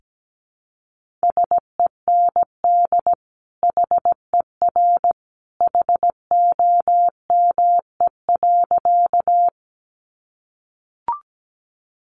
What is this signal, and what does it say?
Morse code, heard as a keyed tone, about 17 wpm, 700 Hz. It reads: SEND HER HOME.
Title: Sentences from Top 1500 Words 17wpm